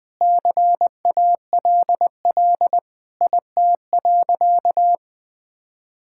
Morse code, heard as a keyed tone, about 20 wpm, 700 Hz. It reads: CALL IT.
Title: Sentences with 7-Word Phrases - 20wpm